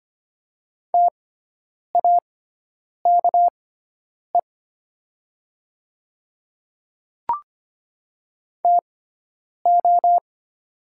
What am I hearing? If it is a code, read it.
Morse code, heard as a keyed tone, about 25 wpm, spaced as for 9 wpm, 690 Hz. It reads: TAKE TO
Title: 6x Character Spacing - Top 200 Words - 25wpm